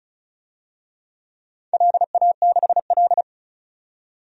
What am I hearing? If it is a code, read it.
Morse code, heard as a keyed tone, about 34 wpm, 700 Hz. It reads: LA6L
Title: DXCC-100 Call Signs - 4 Characters - 35wpm